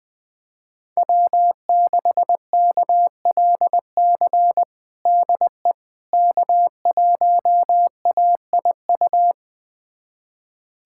Morse code, heard as a keyed tone, about 20 wpm, 700 Hz. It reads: W6KLC DE K1AIU
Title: USA Callsign DE Callsign - 5 Characters - 20wpm